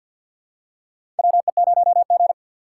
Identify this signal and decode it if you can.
Morse code, heard as a keyed tone, about 47 wpm, 700 Hz. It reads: WE0G
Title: USA Call Signs - 4 Characters - 2x Character Spacing - 50wpm